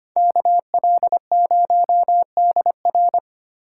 Morse code, heard as a keyed tone, about 29 wpm, 700 Hz. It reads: KL0DR
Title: Speed Racing - USA 2x2 Call Signs 25wpm